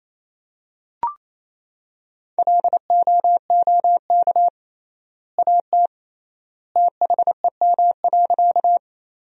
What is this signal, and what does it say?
Morse code, heard as a keyed tone, about 28 wpm, 700 Hz. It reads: LOOK AT THEM.
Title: 3x Word-Spacing - Sentences from Top 100 Words - 28wpm